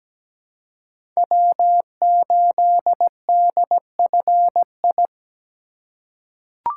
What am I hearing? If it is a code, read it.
Morse code, heard as a keyed tone, about 17 wpm, 700 Hz. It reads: W8DFI
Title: USA Call Signs - 5 Characters - 3x Character Spacing - 17wpm